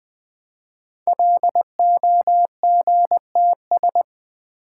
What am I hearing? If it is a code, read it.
Morse code, heard as a keyed tone, about 20 wpm, 700 Hz. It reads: LOGTS
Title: Code Groups - 5 Letter:Numbers - 20wpm